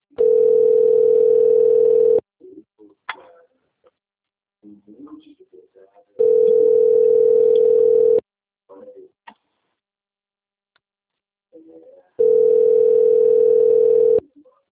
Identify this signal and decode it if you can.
Ringback tone (North America), 440 + 480 Hz, with speech.
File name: rg-1600-unknown- - - .41591